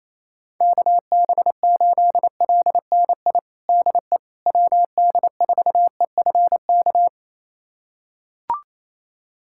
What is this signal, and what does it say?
Morse code, heard as a keyed tone, about 28 wpm, 700 Hz. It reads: KB8LNI DE WD4EFK
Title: USA Callsign DE Callsign 28wpm